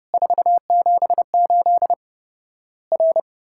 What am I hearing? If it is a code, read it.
Morse code, heard as a keyed tone, about 32 wpm, 700 Hz. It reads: 478 R
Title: Worked All Europe DX Contest - QTC Exchange - 30wpm